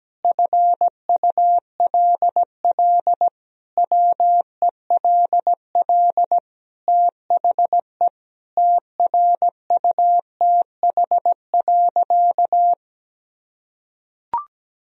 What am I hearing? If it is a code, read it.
Morse code, heard as a keyed tone, about 17 wpm, 700 Hz. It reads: FULL WELL THE TRUTH.
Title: Sentences from Top 1000 Words - Review Part 3 - 17wpm